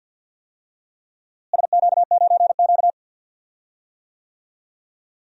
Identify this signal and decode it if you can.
Morse code, heard as a keyed tone, about 48 wpm, 700 Hz. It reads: SQ9Q
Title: DXCC-100 Call Signs - 4 Characters - Repeated 3x - 50wpm